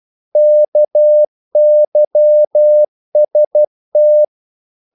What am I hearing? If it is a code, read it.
Morse code, heard as a keyed tone, about 12 wpm, 600 Hz. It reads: KYST
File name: Korte ord DK 12wpm